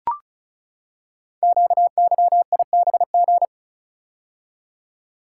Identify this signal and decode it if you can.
Morse code, heard as a keyed tone, about 36 wpm, 690 Hz. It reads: QYIBG
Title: Code Groups - 5 Letters - 35wpm